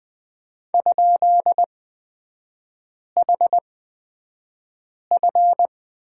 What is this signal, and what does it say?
Morse code, heard as a keyed tone, about 20 wpm, spaced as for 5 wpm, 700 Hz. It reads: ?HF
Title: 224 Lesson 36e - BT - Three Characters 20wpm